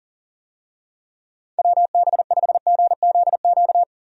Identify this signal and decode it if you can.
Morse code, heard as a keyed tone, about 40 wpm, 700 Hz. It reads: WB5GZQ
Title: USA Call Signs - 6 Characters - 40wpm